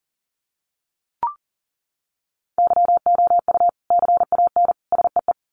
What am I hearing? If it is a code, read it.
Morse code, heard as a keyed tone, about 40 wpm, 700 Hz. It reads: YOU CAN SEE
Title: Sets of 3 Words 40wpm